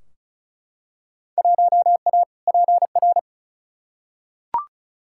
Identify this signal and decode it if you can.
Morse code, heard as a keyed tone, about 36 wpm, 700 Hz. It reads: 1A PR
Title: Field Day Exchanges Encore - 35wpm